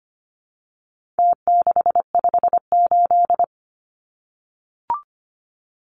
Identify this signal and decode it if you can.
Morse code, heard as a keyed tone, about 25 wpm, 700 Hz. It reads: T658
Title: Numbers - Contest Style - 4-Digit 25wpm